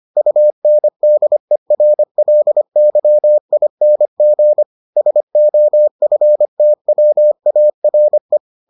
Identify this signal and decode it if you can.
Morse code, heard as a keyed tone, about 25 wpm, 600 Hz. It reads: UNDERLYING SOFTWARE